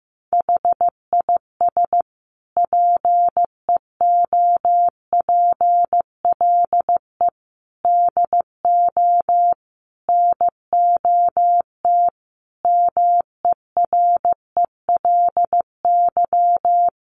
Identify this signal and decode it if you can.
Morse code, heard as a keyed tone, about 15 wpm, 700 Hz. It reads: HIS PEOPLE DO NOT MERELY
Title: Sentences from Top 1400 Words 15wpm